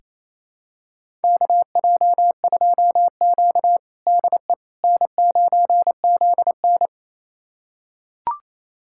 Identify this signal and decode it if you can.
Morse code, heard as a keyed tone, about 28 wpm, 700 Hz. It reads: KJ2Q DE N9ZN
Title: USA Callsign DE Callsign - 4 Characters - 28wpm